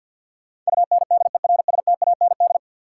Morse code, heard as a keyed tone, about 52 wpm, 700 Hz. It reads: UNDERSTAND
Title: Sentences from Top 600 Words - New Word Drill - 50wpm